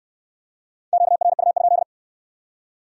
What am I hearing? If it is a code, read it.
Morse code, heard as a keyed tone, about 63 wpm, 700 Hz. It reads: QRU?